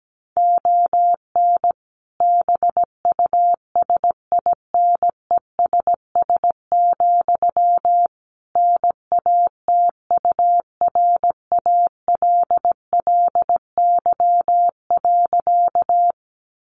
Morse code, heard as a keyed tone, about 17 wpm, 700 Hz. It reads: ON BUSINESS, NATURALLY.
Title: Sentences from Top 1300 Words 17wpm